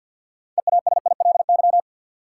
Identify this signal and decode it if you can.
Morse code, heard as a keyed tone, about 52 wpm, 700 Hz. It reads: EASILY
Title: Sentences from Top 900 Words - New Word Drill - 50wpm